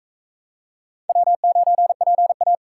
Morse code, heard as a keyed tone, about 40 wpm, 700 Hz. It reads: W9PA